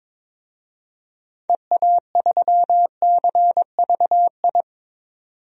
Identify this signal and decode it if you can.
Morse code, heard as a keyed tone, about 21 wpm, 700 Hz. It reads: EA3CVI
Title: DXCC-100 Call Signs - 6 Characters - 22wpm